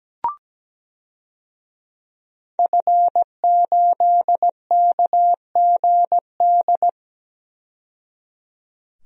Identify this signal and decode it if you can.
Morse code, heard as a keyed tone, about 17 wpm, 700 Hz. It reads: F8KGD
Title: DXCC-100 Call Signs - 5 Characters - 17wpm